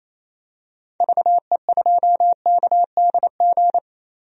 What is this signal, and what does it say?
Morse code, heard as a keyed tone, about 28 wpm, 700 Hz. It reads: VE2KDG
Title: DXCC-100 Call Signs - 6 Characters - 28wpm